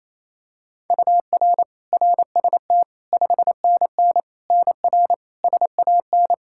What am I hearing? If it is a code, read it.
Morse code, heard as a keyed tone, about 28 wpm, 700 Hz. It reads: UR RST 5NN NR SAN
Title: Practice QSO - Exchange 1 - Beginning - 28wpm